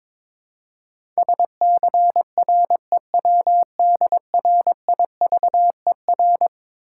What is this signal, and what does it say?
Morse code, heard as a keyed tone, about 22 wpm, 700 Hz. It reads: SCREWDRIVER
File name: Easy Words - Speed-Racing with Hints - 22wpm